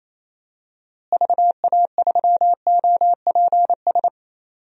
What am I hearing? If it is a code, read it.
Morse code, heard as a keyed tone, about 28 wpm, 700 Hz. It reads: VA3OPS
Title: DXCC-100 Call Signs - 6 Characters - Repeated 2x - 28wpm